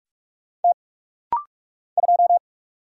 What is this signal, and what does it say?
Morse code, heard as a keyed tone, about 44 wpm, 700 Hz. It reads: T 2
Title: Single Letter-Number - Rapid-Fire 45wpm